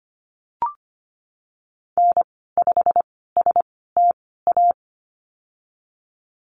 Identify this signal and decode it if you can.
Morse code, heard as a keyed tone, about 25 wpm, spaced as for 16 wpm, 700 Hz. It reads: N5STA
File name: Lesson 15g - Number 5 - Call Signs 25wpm